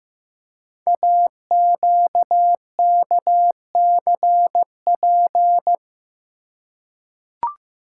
Morse code, heard as a keyed tone, about 15 wpm, 700 Hz. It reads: AQKCP
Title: Code Groups - 5 Letters - 15wpm